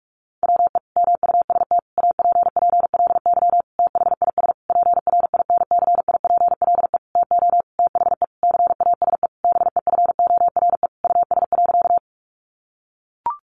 Morse code, heard as a keyed tone, about 44 wpm, 700 Hz. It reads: WE MUST APPLY THIS PRINCIPLE TO THE CASE BEFORE US.
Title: Sentences from Top 1400 Words 45wpm